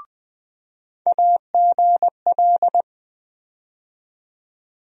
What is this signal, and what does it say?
Morse code, heard as a keyed tone, about 20 wpm, 700 Hz. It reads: AGL
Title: USA Call Sign Postfixes - 3 Letters 20wpm